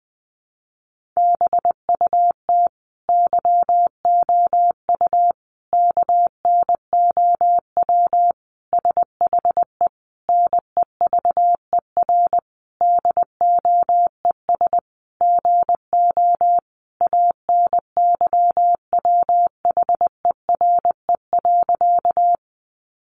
Morse code, heard as a keyed tone, about 20 wpm, 700 Hz. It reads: BUT YOU KNOW SHE NEVER DOES GO ANYWHERE.
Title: Sentences from Top 1400 Words 20wpm